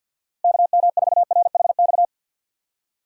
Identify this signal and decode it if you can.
Morse code, heard as a keyed tone, about 52 wpm, 700 Hz. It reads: KM4RHX
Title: USA Call Signs - 6 Characters - Repeated 3x - 50wpm